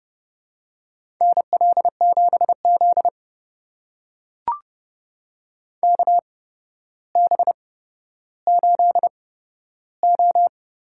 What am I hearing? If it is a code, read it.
Morse code, heard as a keyed tone, about 30 wpm, spaced as for 13 wpm, 700 Hz. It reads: NL7Z KB8O
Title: USA Call Signs - 8x Character Spacing - 30wpm